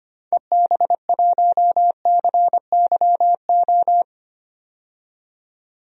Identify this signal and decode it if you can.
Morse code, heard as a keyed tone, about 25 wpm, 700 Hz. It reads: EB1CYO
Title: DXCC-100 Call Signs - 6 Characters - 25wpm